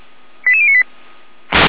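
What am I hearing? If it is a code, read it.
Five-tone selective call (ZVEI), heard as quick runs of five tones, tones of about 70 ms. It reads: 88098
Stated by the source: ZVEI five-tone call received by SDR.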